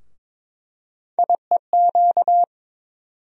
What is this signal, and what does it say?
Morse code, heard as a keyed tone, about 22 wpm, 700 Hz. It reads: IEQ
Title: USA Call Sign Postfixes - 3 Letters 22wpm